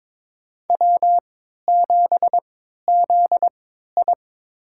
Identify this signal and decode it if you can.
Morse code, heard as a keyed tone, about 22 wpm, spaced as for 13 wpm, 700 Hz. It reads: W7ZI
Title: USA Call Signs - 4 Characters - 3x Character Spacing - 22wpm